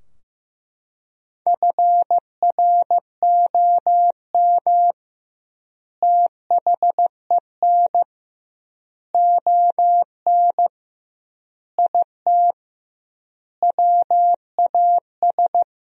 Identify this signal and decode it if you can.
Morse code, heard as a keyed tone, about 16 wpm, 700 Hz. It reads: FROM THEN ON IT WAS
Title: 2x Word-Spacing - Sentences from Top 400 Words - 15wpm